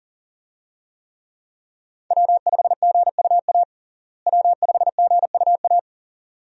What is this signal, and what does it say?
Morse code, heard as a keyed tone, about 40 wpm, 700 Hz. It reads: W5GUA W5GUA
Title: USA Call Signs - 5 Characters - Repeated 2x - 40wpm